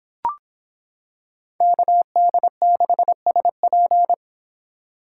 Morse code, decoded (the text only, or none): KD6SP